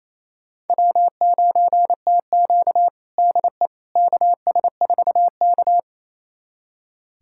Morse code, heard as a keyed tone, about 28 wpm, 700 Hz. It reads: W9TQ DE KS4K
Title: USA Callsign DE Callsign - 4 Characters - 28wpm